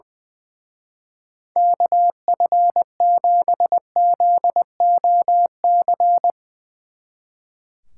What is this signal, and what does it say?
Morse code, heard as a keyed tone, about 22 wpm, 700 Hz. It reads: KF7ZOC
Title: Speed Racing - USA 2x3 Call Signs 20wpm